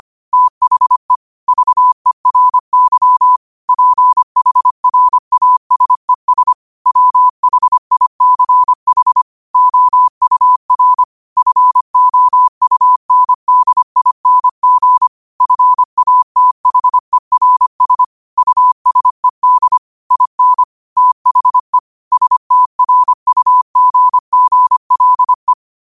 Morse code, heard as a keyed tone, about 25 wpm, 1000 Hz. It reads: THE VERY PHRASES WHICH OUR FOUNDING FATHERS USED IN THE STRUGGLE
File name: qotdfiles / 25wpm